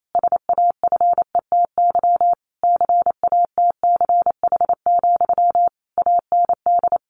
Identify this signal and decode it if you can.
Morse code, heard as a keyed tone, about 28 wpm, 700 Hz. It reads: SAFETY CATCH, AND
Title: A Sound Of Thunder 28wpm